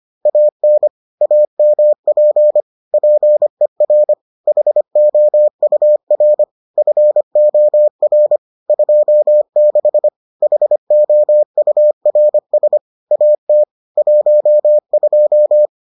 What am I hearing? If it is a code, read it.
Morse code, heard as a keyed tone, about 25 wpm, 600 Hz. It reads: AN AMP PER HOUR FOR 26 HOURS AT 12